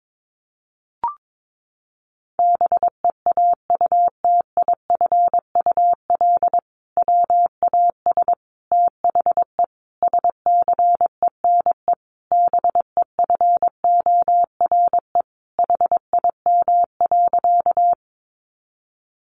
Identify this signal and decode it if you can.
Morse code, heard as a keyed tone, about 22 wpm, 700 Hz. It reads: BEAUTIFUL WAS THE SCENE BEFORE HIM.
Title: Sentences from Top 1100 Words 22wpm